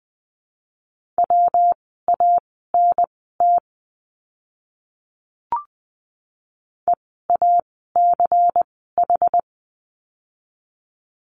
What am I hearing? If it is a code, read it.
Morse code, heard as a keyed tone, about 20 wpm, spaced as for 15 wpm, 700 Hz. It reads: WANT EACH
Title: 206 Lesson 33f - Letter j - Words 20wpm